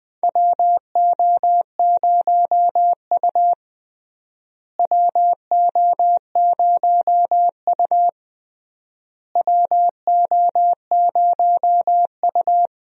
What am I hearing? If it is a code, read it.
Morse code, heard as a keyed tone, about 19 wpm, 700 Hz. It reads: WO0U WO0U WO0U
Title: USA Call Signs - 4 Characters - Repeated 3x - 20wpm